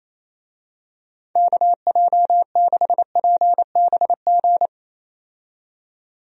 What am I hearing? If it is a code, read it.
Morse code, heard as a keyed tone, about 28 wpm, 700 Hz. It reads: KJ6PBG
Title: USA Call Signs - 6 Characters - 28wpm